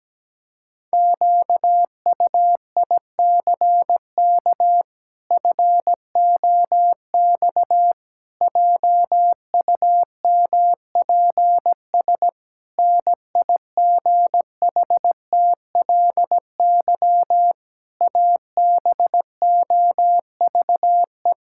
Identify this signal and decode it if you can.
Morse code, heard as a keyed tone, about 17 wpm, 700 Hz. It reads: QUICK FOX JUMPS NIGHTLY ABOVE
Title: Pangram Challenge v2 - 17wpm